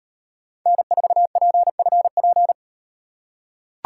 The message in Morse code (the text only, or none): N4PFP